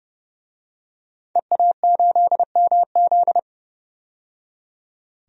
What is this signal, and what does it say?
Morse code, heard as a keyed tone, about 31 wpm, 700 Hz. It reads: EA8MZ
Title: DXCC-100 Call Signs - 5 Characters - 30wpm